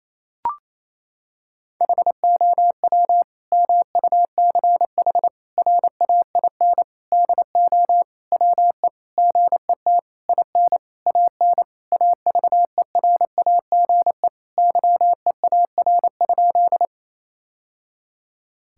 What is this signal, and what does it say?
Morse code, heard as a keyed tone, about 28 wpm, 700 Hz. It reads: HOW MUCH RAIN DO WE GET IN AN AVERAGE YEAR?
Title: Sentences from Top 1000 Words - Review Part 6 - 28wpm